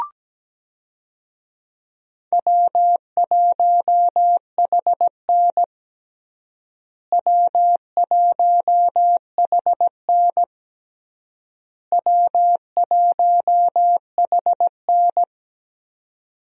Morse code, heard as a keyed tone, about 17 wpm, 700 Hz. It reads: W1HN W1HN W1HN
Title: USA Call Signs - 4 Characters - Repeated 3x - 17wpm